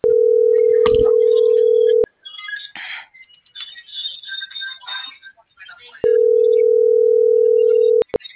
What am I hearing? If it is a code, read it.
Ringback tone (North America), 440 + 480 Hz.